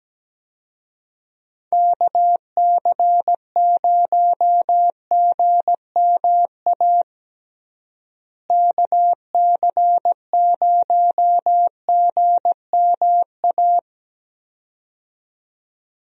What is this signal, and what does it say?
Morse code, heard as a keyed tone, about 17 wpm, 700 Hz. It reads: KC0GMA KC0GMA
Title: USA Call Signs - 6 Characters - Repeated 2x - 17wpm